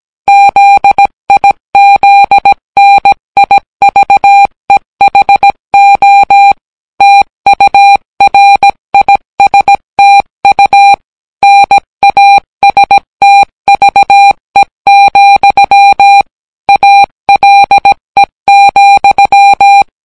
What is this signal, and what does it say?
Morse code, heard as a keyed tone, about 18 wpm, 800 Hz. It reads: ZIZNIVEHO TURISTU NASTVE, ALE,